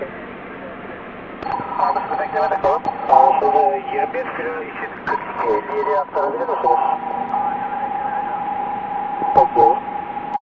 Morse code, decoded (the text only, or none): none